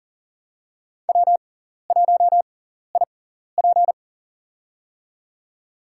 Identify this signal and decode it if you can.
Morse code, heard as a keyed tone, about 40 wpm, spaced as for 14 wpm, 700 Hz. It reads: W1IP
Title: USA Call Signs - 4 Characters - 6x Character Spacing - 40wpm